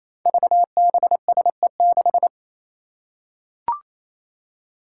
Morse code, heard as a keyed tone, about 28 wpm, 700 Hz. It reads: VBSE6
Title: Five Challenging Letter-Number Combinations 28wpm